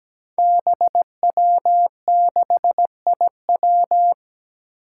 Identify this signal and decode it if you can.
Morse code, heard as a keyed tone, about 17 wpm, 700 Hz. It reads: BW6IW
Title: Five Challenging Letter-Number Combinations 17wpm